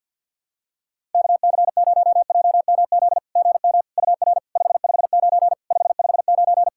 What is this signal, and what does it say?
Morse code, heard as a keyed tone, about 50 wpm, 700 Hz. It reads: KC0JMZ GM UR 559 559
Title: SOTA QSO Practice 50wpm